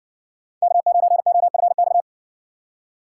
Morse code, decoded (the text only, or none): K9OFX